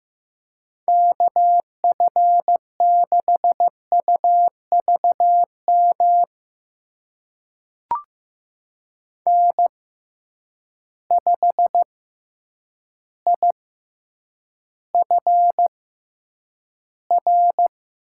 Morse code, heard as a keyed tone, about 15 wpm, spaced as for 8 wpm, 700 Hz. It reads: KF6UVM N5IFR
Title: USA Call Signs - 6x Character Spacing - 15wpm